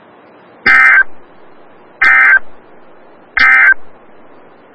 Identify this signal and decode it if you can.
Emergency Alert System (SAME) tones, heard as harsh buzzing bursts. It is heard from 0.5 to 3.8 s.
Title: Des Moines, IA NOAA Weather Radio - WXL57